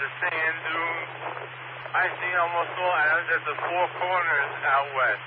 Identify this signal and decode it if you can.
Emergency Alert System (SAME) tones, heard as harsh buzzing bursts, with speech. It is heard from 0.7 to 4.6 s.